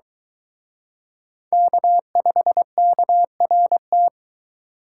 Morse code, heard as a keyed tone, about 24 wpm, 700 Hz. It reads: K5KRT